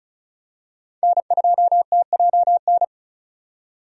Morse code, heard as a keyed tone, about 36 wpm, 700 Hz. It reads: N2TJN
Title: USA Call Signs - 5 Characters - 35wpm